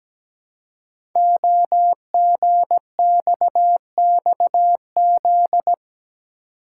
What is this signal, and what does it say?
Morse code, heard as a keyed tone, about 17 wpm, 700 Hz. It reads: OGXXZ
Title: Code Groups - 5 Letter:Numbers - 17wpm